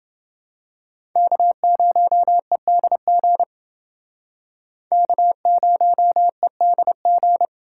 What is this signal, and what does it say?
Morse code, heard as a keyed tone, about 29 wpm, 700 Hz. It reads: K0EDG K0EDG